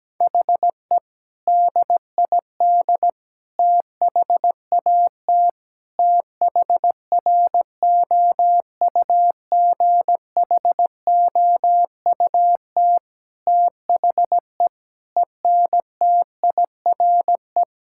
Morse code, heard as a keyed tone, about 17 wpm, 700 Hz. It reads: HE DID THAT THROUGHOUT THE ENTIRE
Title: Sentences from Top 1000 Words - Review Part 5 - 17wpm